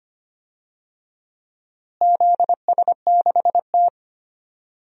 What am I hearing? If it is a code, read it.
Morse code, heard as a keyed tone, about 25 wpm, 700 Hz. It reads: ZS6T
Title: DXCC-100 Call Signs - 4 Characters - Repeated 3x - 25wpm